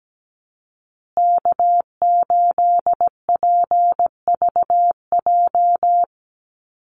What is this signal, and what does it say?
Morse code, heard as a keyed tone, about 17 wpm, 700 Hz. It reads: K8PVJ
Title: USA Call Signs Repeated 3x 17wpm